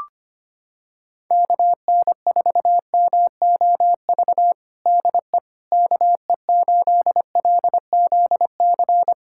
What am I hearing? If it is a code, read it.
Morse code, heard as a keyed tone, about 25 wpm, 700 Hz. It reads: KN4MOV DE KE8LZC
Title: USA Callsign DE Callsign - 6 Characters - 25wpm